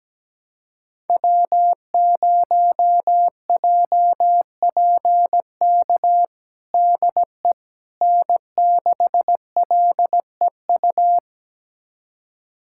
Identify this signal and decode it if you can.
Morse code, heard as a keyed tone, about 17 wpm, 700 Hz. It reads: W0JPK DE N6LEU
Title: USA Callsign DE Callsign - 5 Characters - 17wpm